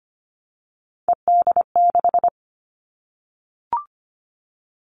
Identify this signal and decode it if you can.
Morse code, heard as a keyed tone, about 25 wpm, 700 Hz. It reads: ED6
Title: Callsign Prefixes for Top 100 DXCC - 25wpm